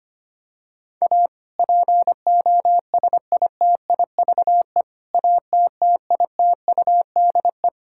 Morse code, heard as a keyed tone, about 25 wpm, 700 Hz. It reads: A POSITIVE ATTITUDE
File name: practice / 25wpm / Difficult Sentences with Hints - 25wpm